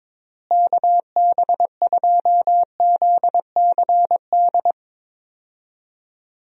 Morse code, heard as a keyed tone, about 22 wpm, 700 Hz. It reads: KB2ZCD
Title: USA Call Signs - 6 Characters - Repeated 2x - 22wpm